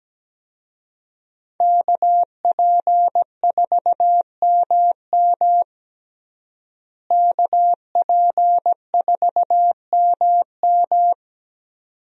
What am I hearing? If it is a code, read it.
Morse code, heard as a keyed tone, about 17 wpm, 700 Hz. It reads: KP4MM KP4MM
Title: USA Call Signs - 5 Characters - Repeated 2x - 17wpm